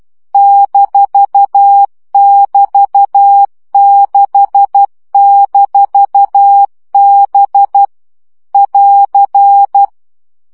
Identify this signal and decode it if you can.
Morse code, heard as a keyed tone, about 12 wpm, 800 Hz. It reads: -<BT>6-B <AR>
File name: CW gyakorlas b 6 egyenlo minusz 10 baud